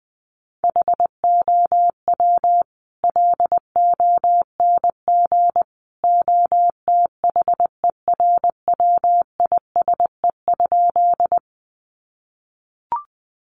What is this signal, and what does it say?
Morse code, heard as a keyed tone, about 20 wpm, 700 Hz. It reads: HOW LONG OTHERWISE?
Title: Sentences from Top 1200 Words 20wpm